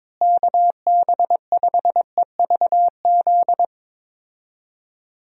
Morse code, heard as a keyed tone, about 22 wpm, 700 Hz. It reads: KB5EVZ
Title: USA Call Signs - 6 Characters - Repeated 2x - 22wpm